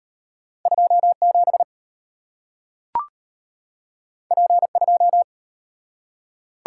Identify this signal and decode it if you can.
Morse code, heard as a keyed tone, about 33 wpm, 690 Hz. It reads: 27 P2